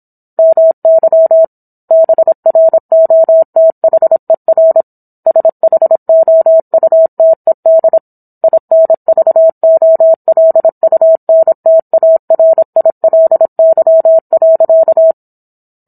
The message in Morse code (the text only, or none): MY BROTHER SHOUTED INVOLUNTARILY.